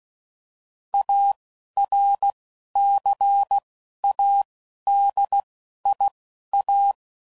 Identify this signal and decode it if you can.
Morse code, heard as a keyed tone, about 16 wpm, spaced as for 11 wpm, 800 Hz. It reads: ARCADIA
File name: CW Code Groups @10WPM / fromSM's / Cities